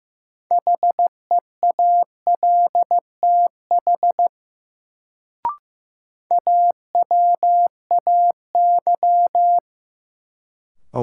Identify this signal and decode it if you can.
Morse code, heard as a keyed tone, about 15 wpm, 700 Hz. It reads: HEALTH AWAY
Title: Sentences from Top 600 Words - New Word Drill - 15wpm